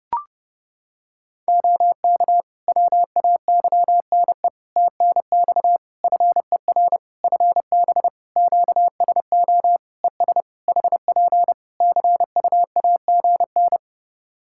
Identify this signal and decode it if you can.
Morse code, heard as a keyed tone, about 30 wpm, 700 Hz. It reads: OK WAYNE TNX FER FB QSO ES HP CUAGN
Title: Practice QSO - Exchange 4 - Ending - 30wpm